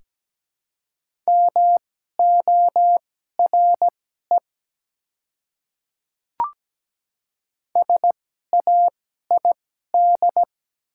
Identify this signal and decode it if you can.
Morse code, heard as a keyed tone, about 17 wpm, spaced as for 12 wpm, 700 Hz. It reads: MORE SAID
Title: 2x Character Spacing - Top 300 Words - 17wpm